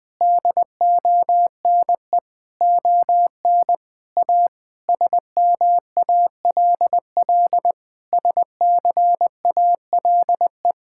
Morse code, heard as a keyed tone, about 21 wpm, 700 Hz. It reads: DONE ON A SMALL SCALE
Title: Sentences from Top 1000 Words - Review Part 5 - 20wpm